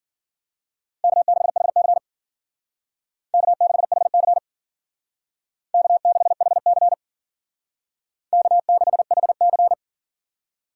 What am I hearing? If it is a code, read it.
Morse code, heard as a keyed tone, about 51 wpm, 700 Hz. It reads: K6HC K6HC K6HC K6HC